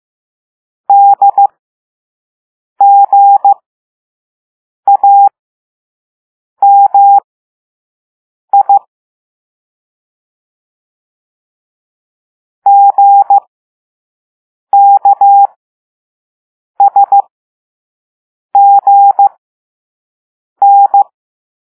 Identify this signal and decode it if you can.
Morse code, heard as a keyed tone, about 15 wpm, spaced as for 5 wpm, 800 Hz. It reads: DGAMI GKSGN